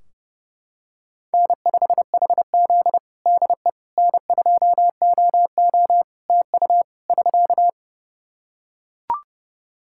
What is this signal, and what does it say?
Morse code, heard as a keyed tone, about 30 wpm, 700 Hz. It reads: N5HZ DE N2OO TU <SK>
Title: Practice QSO - Exchange 4 - Ending - 30wpm